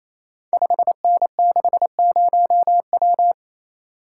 Morse code, heard as a keyed tone, about 29 wpm, 700 Hz. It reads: 5N60W
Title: Code Groups - 5 Letter:Numbers - 28wpm